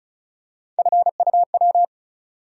Morse code, heard as a keyed tone, about 36 wpm, 700 Hz. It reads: FUW
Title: USA Call Sign Postfixes - 3 Letters 35wpm